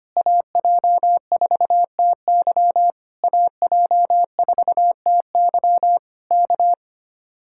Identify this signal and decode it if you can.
Morse code, heard as a keyed tone, about 24 wpm, 700 Hz. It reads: AJ4TY AJ4TY K
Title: Speed-Racing Common QSO Phrases - 25wpm